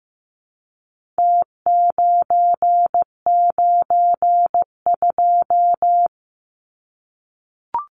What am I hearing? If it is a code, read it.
Morse code, heard as a keyed tone, about 15 wpm, 700 Hz. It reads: T992
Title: Numbers - Contest Style - 4-Digit 15wpm